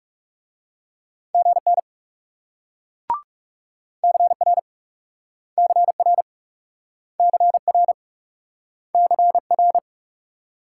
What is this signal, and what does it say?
Morse code, heard as a keyed tone, about 42 wpm, 700 Hz. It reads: GN CR CR CR CR